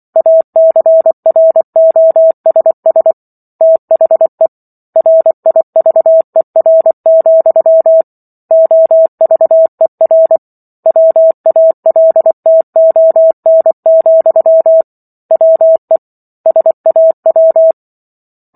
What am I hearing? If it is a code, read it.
Morse code, heard as a keyed tone, about 24 wpm, 650 Hz. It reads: ACROSS THE RIVER, OVER WALTON, WE SAW
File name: Book One / 15-Chapter 15 - 24 WPM